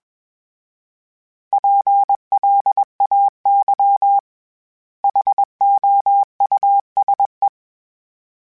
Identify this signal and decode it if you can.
Morse code, heard as a keyed tone, about 21 wpm, 800 Hz. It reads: PLAY HOUSE
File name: General / General-mp3s / Houses @20